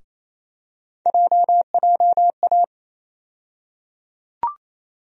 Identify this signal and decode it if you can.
Morse code, heard as a keyed tone, about 28 wpm, 700 Hz. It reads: JJA
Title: USA Call Sign Postfixes - 3 Letters 28wpm